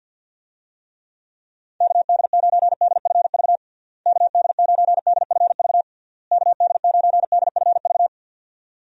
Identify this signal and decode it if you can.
Morse code, heard as a keyed tone, about 50 wpm, 700 Hz. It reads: KD9DFV KD9DFV KD9DFV
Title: USA Call Signs - 6 Characters - Repeated 3x - 50wpm